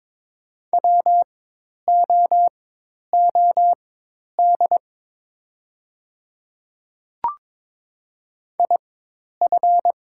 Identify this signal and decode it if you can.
Morse code, heard as a keyed tone, about 22 wpm, spaced as for 10 wpm, 700 Hz. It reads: WOOD IF
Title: 4x Character Spacing - Top 300 Words - 22wpm